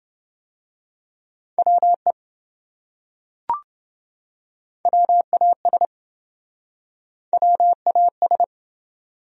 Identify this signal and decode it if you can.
Morse code, heard as a keyed tone, about 27 wpm, 700 Hz. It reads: WE WAS WAS